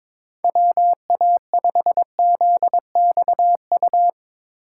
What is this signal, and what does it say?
Morse code, heard as a keyed tone, about 22 wpm, 700 Hz. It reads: WA5ZXU
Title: Speed Racing - USA 2x3 Call Signs 22wpm